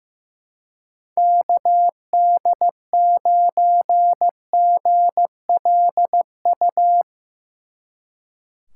Morse code, heard as a keyed tone, about 16 wpm, 700 Hz. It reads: KD9GLU
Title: Speed Racing - USA 2x3 Call Signs 15wpm